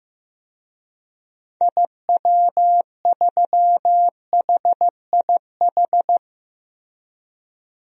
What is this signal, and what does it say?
Morse code, heard as a keyed tone, about 15 wpm, 700 Hz. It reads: IW3HIH
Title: DXCC-100 Call Signs - 6 Characters - Repeated 2x - 15wpm